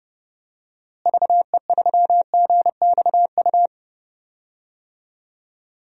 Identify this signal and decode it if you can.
Morse code, heard as a keyed tone, about 30 wpm, 700 Hz. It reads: VE3GXU
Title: DXCC-100 Call Signs - 6 Characters - Repeated 2x - 30wpm